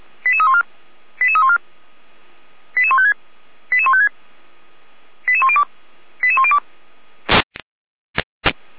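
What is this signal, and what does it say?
Five-tone selective call (ZVEI), heard as quick runs of five tones, tones of about 70 ms. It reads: 88314 88314 88156 88156 88102 88102
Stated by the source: Received on an SDR; ZVEI five-tone code 88103.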